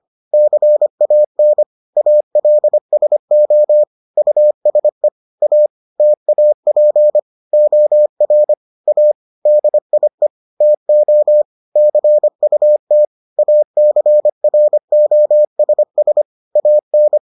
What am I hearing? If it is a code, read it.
Morse code, heard as a keyed tone, about 25 wpm, 600 Hz. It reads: CAN ALSO USE A TAP OR A DIE TO CUT ACROSS AN